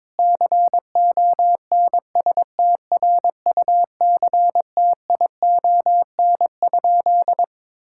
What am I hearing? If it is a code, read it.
Morse code, heard as a keyed tone, about 21 wpm, 700 Hz. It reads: CONSTRUCTION?
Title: Sentences from Top 1600 Words 22wpm